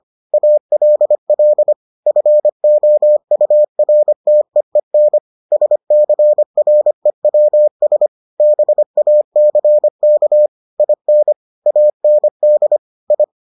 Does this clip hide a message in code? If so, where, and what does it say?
Yes: Morse code from 0.3 to 13.2 s, heard as a keyed tone, about 25 wpm, 600 Hz. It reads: ALL FOURTEEN SCREWS BACK IN AND I